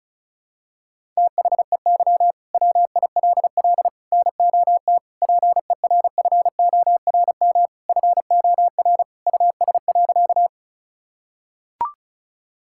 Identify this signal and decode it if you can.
Morse code, heard as a keyed tone, about 36 wpm, 700 Hz. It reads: THEY WILL NOT PERFORM FOR US.
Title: Sentences from Top 1000 Words - Review Part 5 - 35wpm